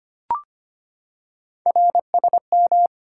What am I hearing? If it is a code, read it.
Morse code, heard as a keyed tone, about 25 wpm, 700 Hz. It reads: RSM
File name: Three Letter-Number Combinations - 25wpm